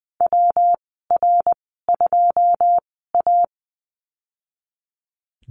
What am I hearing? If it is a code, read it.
Morse code, heard as a keyed tone, about 19 wpm, spaced as for 14 wpm, 700 Hz. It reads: WR2A
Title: 122 Lesson 20g - Number 3 - Call Signs 20wpm